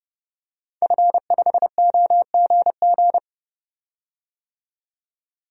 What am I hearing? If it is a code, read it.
Morse code, heard as a keyed tone, about 30 wpm, 700 Hz. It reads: F5OGG
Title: DXCC-100 Call Signs - 5 Characters - 30wpm